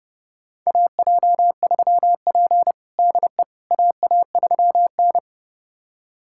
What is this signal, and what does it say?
Morse code, heard as a keyed tone, about 30 wpm, 700 Hz. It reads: AJ3P DE AA3N
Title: USA Callsign DE Callsign - 4 Characters - 30wpm